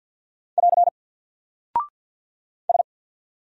Morse code, heard as a keyed tone, about 48 wpm, 700 Hz. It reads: <AR> S
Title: Single Character (Letter:Number:Punctuation:Prosign) 50wpm